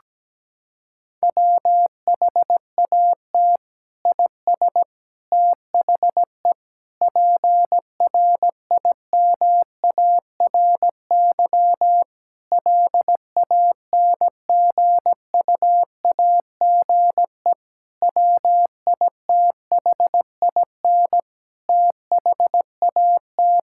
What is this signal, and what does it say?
Morse code, heard as a keyed tone, about 17 wpm, 700 Hz. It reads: WHAT IS THE PRIMARY LANGUAGE WITHIN THAT
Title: Sentences from Top 1000 Words - Review Part 5 - 17wpm